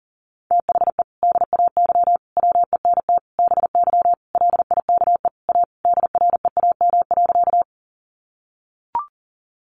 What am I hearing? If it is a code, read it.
Morse code, heard as a keyed tone, about 37 wpm, 700 Hz. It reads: THE DAY WENT BY LIKE A DREAM.
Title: Sentences from Top 1500 Words 40wpm